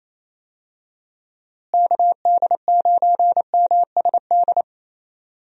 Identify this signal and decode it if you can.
Morse code, heard as a keyed tone, about 28 wpm, 700 Hz. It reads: KD9MSD
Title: USA Call Signs - 6 Characters - Repeated 2x - 28wpm